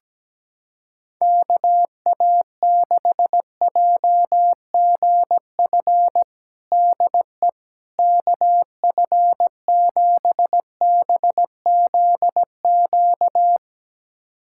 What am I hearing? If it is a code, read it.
Morse code, heard as a keyed tone, about 17 wpm, 700 Hz. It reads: KA6JGF DE KF7BZQ
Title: USA Callsign DE Callsign - 6 Characters - 17wpm